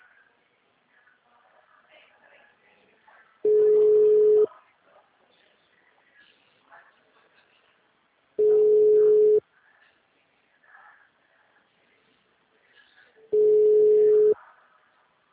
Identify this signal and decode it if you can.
Ringback tone (Europe, 425 Hz).